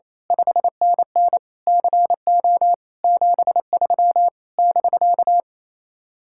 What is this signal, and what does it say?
Morse code, heard as a keyed tone, about 26 wpm, 700 Hz. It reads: 5NN CO 73 <BK>
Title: Speed-Racing Common QSO Phrases - 28wpm